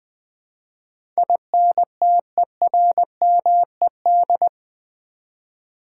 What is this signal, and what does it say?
Morse code, heard as a keyed tone, about 20 wpm, 700 Hz. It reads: INTERMED
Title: Build A Word - 4 Letters At A Time - 12-Letter Words - 20wpm